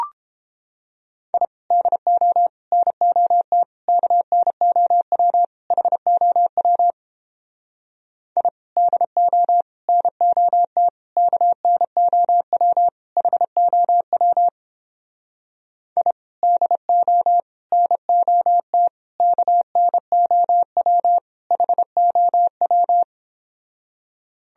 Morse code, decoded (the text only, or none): I DO NOT KNOW HOW I DO NOT KNOW HOW I DO NOT KNOW HOW